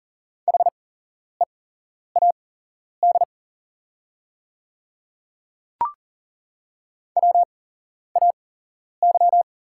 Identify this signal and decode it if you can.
Morse code, heard as a keyed tone, about 40 wpm, spaced as for 11 wpm, 690 Hz. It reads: HEAD WAY
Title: 8x Character Spacing - Top 300 Words - 40wpm